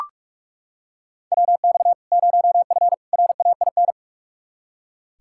Morse code, heard as a keyed tone, about 44 wpm, 700 Hz. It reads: WX 0F RAIN
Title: Practice QSO - Exchange 2 - Info - 45wpm